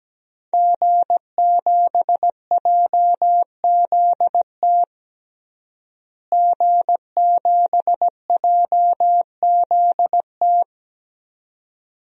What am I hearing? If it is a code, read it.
Morse code, heard as a keyed tone, about 17 wpm, 700 Hz. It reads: G7JZT G7JZT
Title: DXCC-100 Call Signs - 5 Characters - Repeated 2x - 17wpm